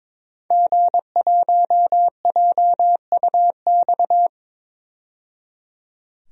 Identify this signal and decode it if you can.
Morse code, heard as a keyed tone, about 22 wpm, 700 Hz. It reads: G1JUX
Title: DXCC-100 Call Signs - 5 Characters - 22wpm